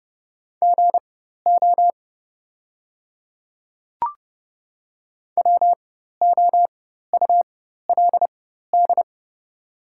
Morse code, heard as a keyed tone, about 30 wpm, spaced as for 14 wpm, 700 Hz. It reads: GO WOULD
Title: 4x Character Spacing - Top 100 Words - 30wpm